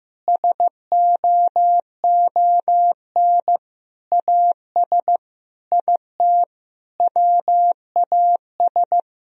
Morse code, heard as a keyed tone, about 15 wpm, 700 Hz. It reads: SOON AS IT WAS
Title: Sentences from Top 1000 Words - Review Part 1 - 15wpm